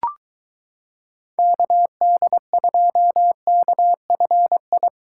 Morse code, heard as a keyed tone, about 22 wpm, 700 Hz. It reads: KD2KFI